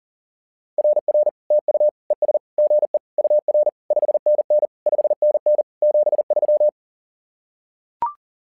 Morse code, heard as a keyed tone, about 40 wpm, 600 Hz. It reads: RR TU ES GE UR 5NN 5NN 73
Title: SOTA QSO Practice 40wpm